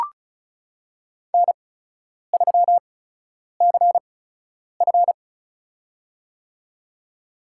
Morse code, heard as a keyed tone, about 36 wpm, spaced as for 10 wpm, 700 Hz. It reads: N3CF
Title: USA Call Signs - 4 Characters - 8x Character Spacing - 35wpm